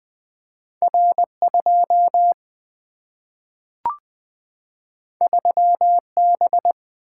Morse code, heard as a keyed tone, about 20 wpm, 700 Hz. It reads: R 2 3 B
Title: Single-Letter Single-Number 20wpm